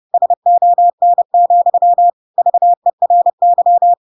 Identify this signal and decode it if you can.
Morse code, heard as a keyed tone, about 30 wpm, 700 Hz. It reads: SON, VERY
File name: Pinocchio / Pinocchio ch3 30 30 ik1hge